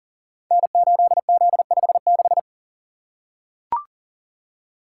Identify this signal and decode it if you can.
Morse code, heard as a keyed tone, about 38 wpm, 700 Hz. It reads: N8Z56
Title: Code Groups - 5 Letter:Numbers - 40wpm